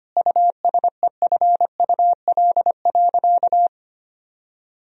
Morse code, heard as a keyed tone, about 25 wpm, 700 Hz. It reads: USEFUL.
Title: Sentences from Top 1000 Words - Review Part 4 - 25wpm